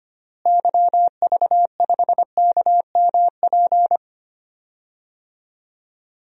Morse code, heard as a keyed tone, about 25 wpm, 700 Hz. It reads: YV5KMP
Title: DXCC-100 Call Signs - 6 Characters - Repeated 2x - 25wpm